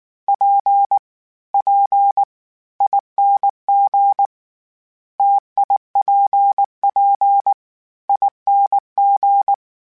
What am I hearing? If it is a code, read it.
Morse code, heard as a keyed tone, about 19 wpm, spaced as for 15 wpm, 800 Hz. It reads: PPING TIPPING
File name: Variable Speed / Variable Speed-mp3s / T AOEI's sp @18